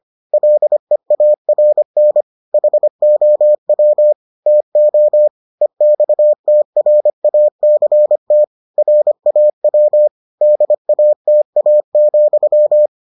Morse code, heard as a keyed tone, about 25 wpm, 600 Hz. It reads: LEARN HOW TO EXTRACT RAW DATA,